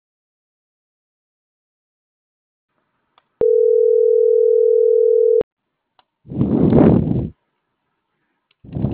Ringback tone (North America), 440 + 480 Hz.